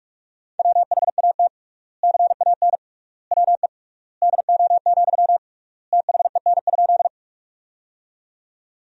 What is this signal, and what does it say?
Morse code, heard as a keyed tone, about 46 wpm, 690 Hz. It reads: WHAT CAN WE DO, THEN?
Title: 3x Word-Spacing - Sentences from Top 100 Words - 45wpm